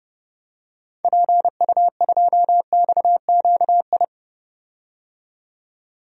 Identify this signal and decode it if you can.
Morse code, heard as a keyed tone, about 32 wpm, 700 Hz. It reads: PU2XQI
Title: DXCC-100 Call Signs - 6 Characters - 30wpm